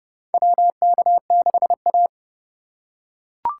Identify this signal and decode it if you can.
Morse code, heard as a keyed tone, about 31 wpm, 700 Hz. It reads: WK6A
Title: USA Call Signs - 4 Characters - 2x Character Spacing - 30wpm